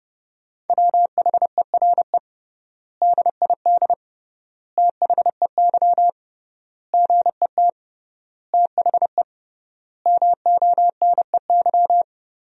Morse code, heard as a keyed tone, about 30 wpm, 700 Hz. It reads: WHERE DID THEY GET THE MONEY
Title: 3x Word-Spacing - Sentences from Top 400 Words - 30wpm